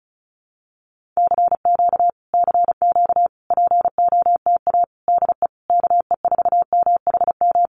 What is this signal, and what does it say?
Morse code, heard as a keyed tone, about 34 wpm, 700 Hz. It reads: CQ CQ POTA DE KE4MHM
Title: Common QSO Phrases Encore 35wpm